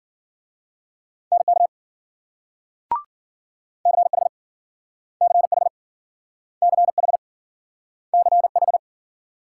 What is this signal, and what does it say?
Morse code, heard as a keyed tone, about 53 wpm, 700 Hz. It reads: NK CH CH CH CH